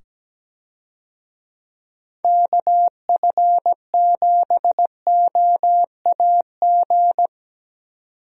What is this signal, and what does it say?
Morse code, heard as a keyed tone, about 17 wpm, 700 Hz. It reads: KF7OAG
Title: USA Call Signs - 6 Characters - 17wpm